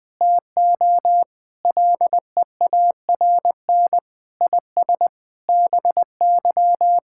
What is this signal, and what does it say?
Morse code, heard as a keyed tone, about 21 wpm, 700 Hz. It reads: TO LEARN IS BY
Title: Sentences with 4-Word Phrases - 20wpm